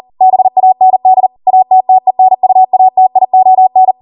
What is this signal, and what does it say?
Morse code, heard as a keyed tone, about 40 wpm, 750 Hz. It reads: BAND ATTENUATION